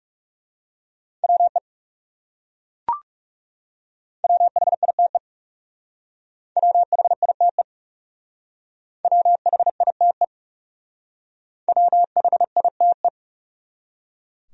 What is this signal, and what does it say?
Morse code, heard as a keyed tone, about 38 wpm, 700 Hz. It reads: WE WHITE WHITE WHITE WHITE